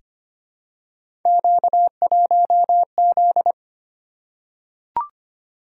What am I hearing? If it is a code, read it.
Morse code, heard as a keyed tone, about 25 wpm, 700 Hz. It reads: Q1Z
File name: Three Letter-Number Combinations - 25wpm